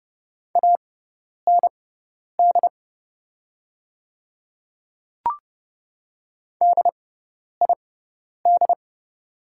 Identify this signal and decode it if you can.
Morse code, heard as a keyed tone, about 30 wpm, spaced as for 10 wpm, 700 Hz. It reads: AND DID
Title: 6x Character Spacing - Top 100 Words - 30wpm